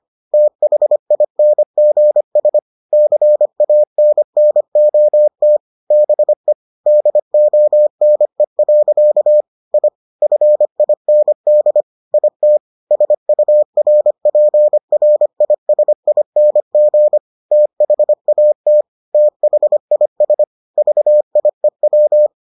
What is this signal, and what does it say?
Morse code, heard as a keyed tone, about 25 wpm, 600 Hz. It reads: THINGS CANNOT BE DONE. I FIND IT SURPRISING THAT THIS VIEW